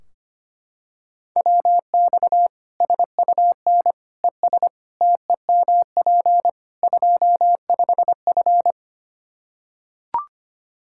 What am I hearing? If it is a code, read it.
Morse code, heard as a keyed tone, about 25 wpm, 700 Hz. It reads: WX SUN ES TEMP 25F
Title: Practice QSO - Exchange 2 - Info - 25wpm